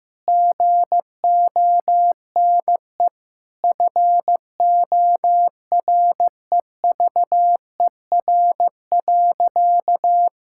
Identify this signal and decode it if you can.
Morse code, heard as a keyed tone, about 16 wpm, 700 Hz. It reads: GONE FOREVER.
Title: Sentences from Top 1600 Words 15wpm